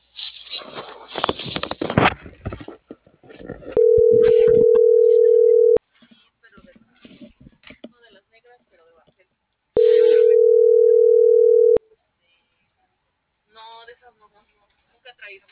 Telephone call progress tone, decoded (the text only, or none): ringback tone